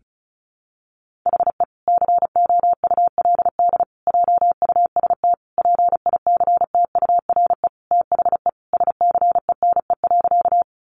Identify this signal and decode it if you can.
Morse code, heard as a keyed tone, about 34 wpm, 700 Hz. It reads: HE COULD JUST PICTURE THE SCENE.
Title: Sentences from Top 1100 Words 35wpm